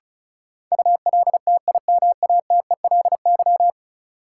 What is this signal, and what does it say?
Morse code, heard as a keyed tone, about 36 wpm, 700 Hz. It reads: ULTIMATELY
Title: Sentences from Top 1200 Words - New Word Drill Alt - 35wpm